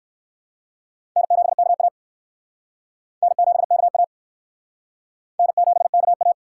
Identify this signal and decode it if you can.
Morse code, heard as a keyed tone, about 53 wpm, 700 Hz. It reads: N7KA N7KA N7KA
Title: Speed Racing - USA 1x2 Call Signs 45wpm